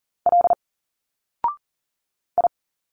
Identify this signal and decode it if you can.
Morse code, heard as a keyed tone, about 40 wpm, 700 Hz. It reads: L I
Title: Single Letters 40wpm